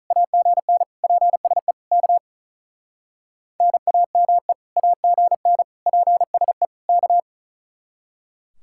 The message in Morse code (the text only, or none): AGN PSE K NAME AGN PSE K